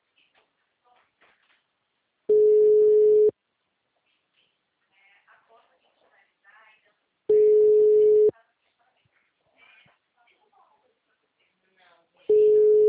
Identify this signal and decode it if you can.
Ringback tone (Europe, 425 Hz), with speech.